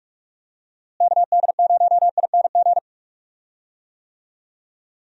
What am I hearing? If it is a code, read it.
Morse code, heard as a keyed tone, about 42 wpm, 700 Hz. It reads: KD0ING
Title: USA Call Signs - 6 Characters - Repeated 2x - 45wpm